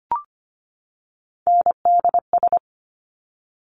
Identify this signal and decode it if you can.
Morse code, heard as a keyed tone, about 25 wpm, 700 Hz. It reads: NDS
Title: Top Three Letter Combinations in Words 25wpm